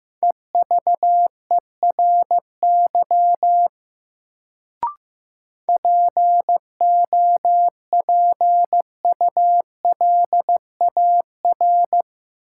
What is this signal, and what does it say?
Morse code, heard as a keyed tone, about 15 wpm, 700 Hz. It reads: EVERY POPULAR
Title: Sets of 5 Words - New Word Drill - 15wpm